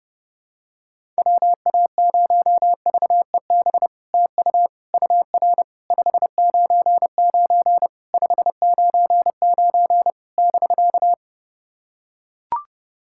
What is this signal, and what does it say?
Morse code, heard as a keyed tone, about 30 wpm, 700 Hz. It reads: WA0VEB TU UR 599 599 <BK>
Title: SOTA QSO Practice 30wpm